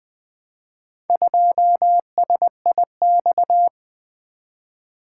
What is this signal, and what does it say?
Morse code, heard as a keyed tone, about 20 wpm, 700 Hz. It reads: 2SIX
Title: Four Letter-Number Combinations - 20wpm